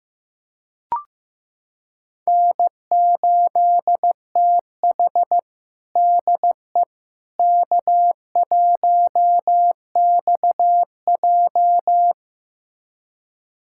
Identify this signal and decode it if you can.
Morse code, heard as a keyed tone, about 15 wpm, 700 Hz. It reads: N8TH DE K1XJ
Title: USA Callsign DE Callsign - 4 Characters - 15wpm